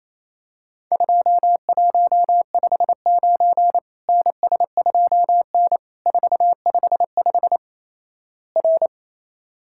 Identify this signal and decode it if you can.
Morse code, heard as a keyed tone, about 28 wpm, 700 Hz. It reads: 2159 NS2N 455 R
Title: Worked All Europe DX Contest - QTC Exchange - 28wpm